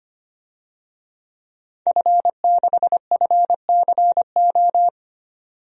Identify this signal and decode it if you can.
Morse code, heard as a keyed tone, about 25 wpm, 700 Hz. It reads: F6FCO
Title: DXCC-100 Call Signs - 5 Characters - 25wpm